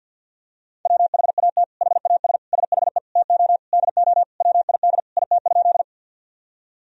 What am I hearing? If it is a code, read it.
Morse code, heard as a keyed tone, about 52 wpm, 700 Hz. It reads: WHAT HAS SHE TO DO WID IT?
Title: Sentences from Top 100 Words and Abbreviations 50wpm